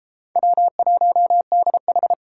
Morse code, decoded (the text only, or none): W1DH